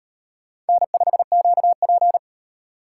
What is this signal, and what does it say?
Morse code, heard as a keyed tone, about 38 wpm, 700 Hz. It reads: N5QP